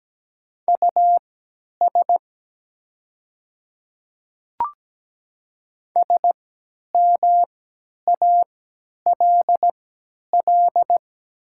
Morse code, heard as a keyed tone, about 17 wpm, spaced as for 10 wpm, 700 Hz. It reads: US SMALL
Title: 3x Character Spacing - Top 200 Words - 17wpm